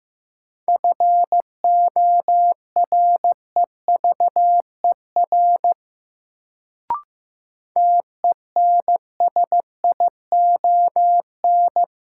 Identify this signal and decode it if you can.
Morse code, heard as a keyed tone, about 15 wpm, 700 Hz. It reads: FOREVER TENSION
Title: Sentences from Top 1600 Words - New Word Drill - 15wpm